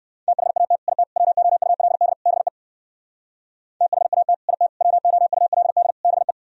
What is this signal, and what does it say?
Morse code, heard as a keyed tone, about 57 wpm, 700 Hz. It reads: THAT IT WOULD BE THAT IT WOULD BE